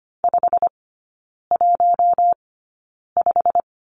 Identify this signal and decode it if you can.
Morse code, heard as a keyed tone, about 25 wpm, spaced as for 9 wpm, 700 Hz. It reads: 515
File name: course / 25:10wpm / 093 Lesson 16e - Letter c - Three Characters 25wpm